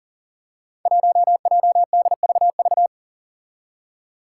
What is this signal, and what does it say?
Morse code, heard as a keyed tone, about 42 wpm, 700 Hz. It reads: 1JDVV
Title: Five Challenging Letter-Number Combinations 40wpm